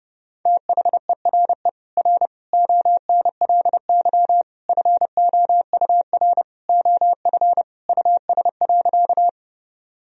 Morse code, heard as a keyed tone, about 30 wpm, 700 Hz. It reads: THERE R ONLY FOUR OF US.
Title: Sentences from Top 300 Words and Abbreviations 30wpm